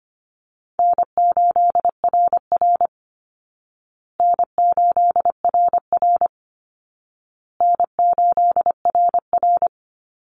Morse code, heard as a keyed tone, about 25 wpm, 700 Hz. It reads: N8RR N8RR N8RR
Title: Call Signs Repeated 2x - 25wpm